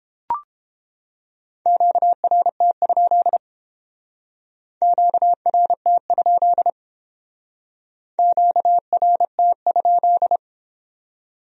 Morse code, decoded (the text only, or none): QRT? QRT? QRT?